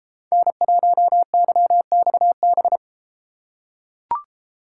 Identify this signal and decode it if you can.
Morse code, heard as a keyed tone, about 33 wpm, 700 Hz. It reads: N1YXB